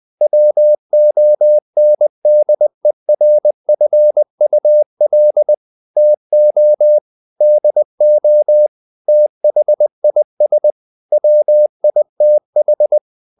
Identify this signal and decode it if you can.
Morse code, heard as a keyed tone, about 20 wpm, 600 Hz. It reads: WONDERFUL TO DO THIS WITH
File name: the old man and the sea / the old man and the sea part 1 20wpm